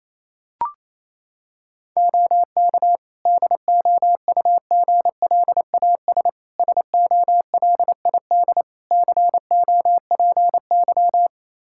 Morse code, heard as a keyed tone, about 28 wpm, 700 Hz. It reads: OK DOUGLAS SOLID COPY
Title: Practice QSO - Exchange 3 - More Info - 28wpm